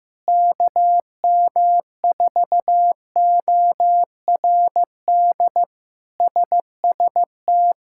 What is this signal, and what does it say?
Morse code, heard as a keyed tone, about 15 wpm, 700 Hz. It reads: KM4ORD SST
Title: K1USN SST QSO Practice - 15wpm